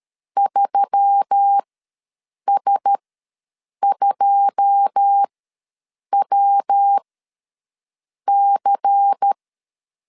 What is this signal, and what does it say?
Morse code, heard as a keyed tone, about 13 wpm, spaced as for 7 wpm, 800 Hz. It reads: 3S2WC
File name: LEZIONI / 10 SIGNS / 10C DOT ws 12wpm 3548